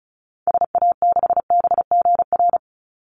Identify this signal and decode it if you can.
Morse code, heard as a keyed tone, about 34 wpm, 700 Hz. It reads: SA6BGR
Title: Call Signs - 35wpm